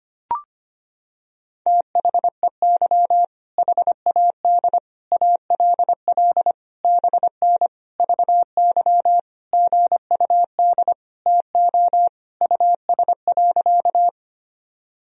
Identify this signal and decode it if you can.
Morse code, heard as a keyed tone, about 25 wpm, 700 Hz. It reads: THEY HAD ALL BN VY GUD TO US.
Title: Sentences from Top 200 Words and Abbreviations 25wpm